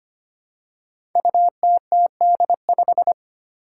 Morse code, heard as a keyed tone, about 25 wpm, 700 Hz. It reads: UTTD5
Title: Five Challenging Letter-Number Combinations 25wpm